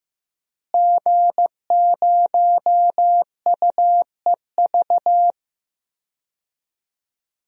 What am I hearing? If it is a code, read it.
Morse code, heard as a keyed tone, about 16 wpm, 700 Hz. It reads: G0UEV
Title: DXCC-100 Call Signs - 5 Characters - Repeated 3x - 15wpm